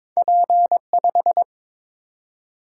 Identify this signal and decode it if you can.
Morse code, heard as a keyed tone, about 22 wpm, 700 Hz. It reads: P5
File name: practice / 22wpm / Two Letter-Number Combinations - 22wpm